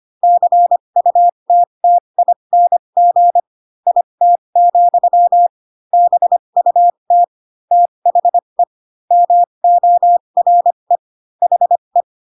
Morse code, heard as a keyed tone, about 25 wpm, spaced as for 22 wpm, 700 Hz. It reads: CUTTING IT, BUT THE MORE HE
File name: Pinocchio / Pinocchio ch3 18 25 ik1hge